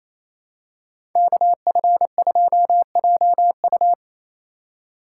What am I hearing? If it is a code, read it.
Morse code, heard as a keyed tone, about 28 wpm, 700 Hz. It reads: KF2JU
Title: USA Call Signs - 3x Character Spacing - 28wpm